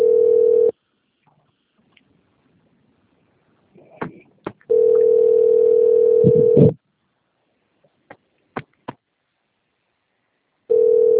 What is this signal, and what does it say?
Ringback tone (North America), 440 + 480 Hz.